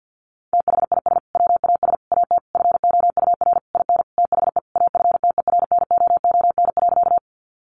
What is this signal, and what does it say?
Morse code, heard as a keyed tone, about 50 wpm, 700 Hz. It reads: THIS WAS AT FOUR IN THE AFTERNOON.
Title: Sentences from Top 1500 Words 50wpm